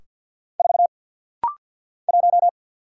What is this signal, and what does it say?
Morse code, heard as a keyed tone, about 50 wpm, 700 Hz. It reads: 4 1
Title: Single Letter-Number - Rapid-Fire 50wpm